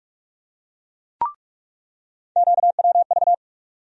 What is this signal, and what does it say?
Morse code, heard as a keyed tone, about 46 wpm, 700 Hz. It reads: QWV
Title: USA Call Sign Postfixes - 3 Letters 45wpm